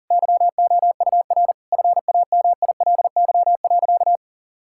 Morse code, heard as a keyed tone, about 40 wpm, 700 Hz. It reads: YOUR FAMILY.
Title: Sentences from Top 1600 Words 40wpm